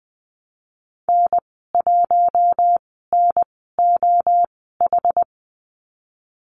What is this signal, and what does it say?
Morse code, heard as a keyed tone, about 20 wpm, spaced as for 14 wpm, 700 Hz. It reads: N1NOH
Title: 147 Lesson 24g - Letter y - Call Signs 20wpm